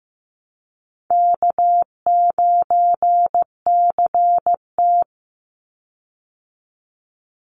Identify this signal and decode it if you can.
Morse code, heard as a keyed tone, about 15 wpm, 700 Hz. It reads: K9CT
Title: Easy Call Signs Repeated 3x - 15wpm